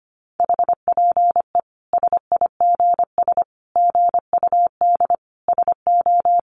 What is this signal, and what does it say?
Morse code, heard as a keyed tone, about 25 wpm, 700 Hz. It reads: HPE SIGS GUD SO
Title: Common QSO Phrases 25wpm